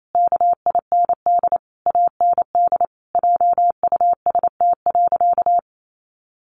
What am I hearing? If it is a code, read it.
Morse code, heard as a keyed tone, about 28 wpm, 700 Hz. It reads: KIND AND JUST.
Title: Sentences from Top 800 Words 28wpm